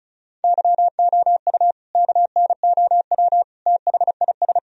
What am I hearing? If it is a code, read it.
Morse code, heard as a keyed tone, about 34 wpm, 700 Hz. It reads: YOU KNOW THIS
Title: Sentences from Top 1000 Words - Review Part 2 - 35wpm